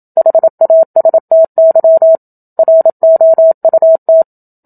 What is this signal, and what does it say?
Morse code, heard as a keyed tone, about 27 wpm, 650 Hz. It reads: HASTY ROUT
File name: Book Two / 01-Chapter 1 - 27 WPM